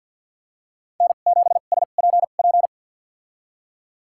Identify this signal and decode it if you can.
Morse code, heard as a keyed tone, about 53 wpm, spaced as for 37 wpm, 700 Hz. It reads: N7SPP
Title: USA Call Signs - 5 Characters - 2x Character Spacing - 50wpm